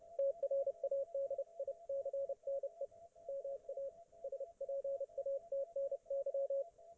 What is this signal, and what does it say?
Morse code, heard as a keyed tone, about 30 wpm, 550 Hz, with radio noise. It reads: TRADICNE MA SPATNY